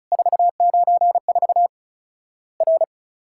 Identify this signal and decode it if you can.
Morse code, heard as a keyed tone, about 36 wpm, 700 Hz. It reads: 494 R
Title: Worked All Europe DX Contest - QTC Exchange - 35wpm